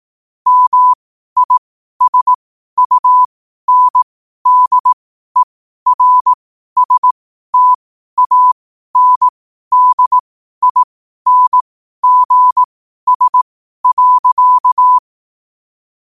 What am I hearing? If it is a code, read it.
Morse code, heard as a keyed tone, about 18 wpm, spaced as for 12 wpm, 1000 Hz. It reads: MISUNDERSTANDINGS.
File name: qotdfiles / 12wpm